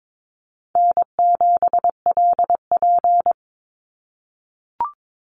Morse code, heard as a keyed tone, about 22 wpm, 700 Hz. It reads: N7LP
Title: NA SOTA Chaser Callsigns 22wpm